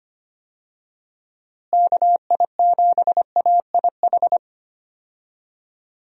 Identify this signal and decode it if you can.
Morse code, heard as a keyed tone, about 25 wpm, 700 Hz. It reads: KI7AIH
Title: USA Call Signs - 6 Characters - Repeated 3x - 25wpm